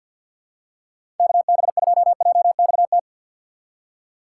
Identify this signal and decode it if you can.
Morse code, heard as a keyed tone, about 52 wpm, 700 Hz. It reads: KB2JXT
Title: USA Call Signs - 6 Characters - 4x Character Spacing - 50wpm